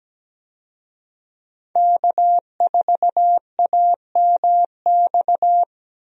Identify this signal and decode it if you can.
Morse code, heard as a keyed tone, about 17 wpm, 700 Hz. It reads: K4AMX
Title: USA Call Signs - 5 Characters - 17wpm